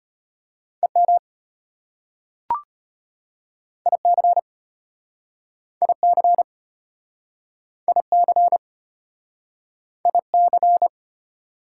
Morse code, heard as a keyed tone, about 36 wpm, 700 Hz. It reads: EM IC IC IC IC